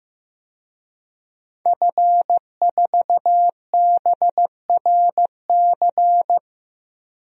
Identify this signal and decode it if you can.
Morse code, heard as a keyed tone, about 15 wpm, 700 Hz. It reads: F4BRC
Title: DXCC-100 Call Signs - 5 Characters - Repeated 2x - 15wpm